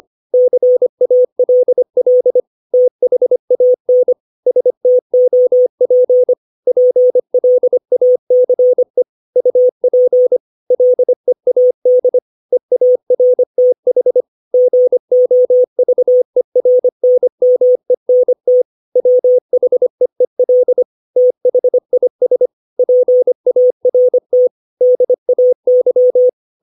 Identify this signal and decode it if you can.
Morse code, heard as a keyed tone, about 25 wpm, 500 Hz. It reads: CALL THAN STOP PLACE UP LEAD EARTH GOVERNMENT WHEEL THIS PART DAY
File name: 632 words / 632 Words 25WPM